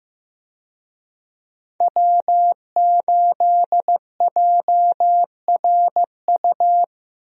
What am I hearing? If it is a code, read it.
Morse code, heard as a keyed tone, about 14 wpm, 700 Hz. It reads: W8JRU
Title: USA Call Signs - 5 Characters - Repeated 3x - 15wpm